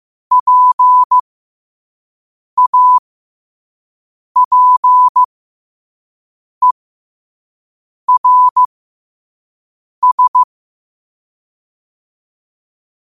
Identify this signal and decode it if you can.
Morse code, heard as a keyed tone, about 15 wpm, spaced as for 5 wpm, 1000 Hz. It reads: PAPERS
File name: qotdfiles / 5wpm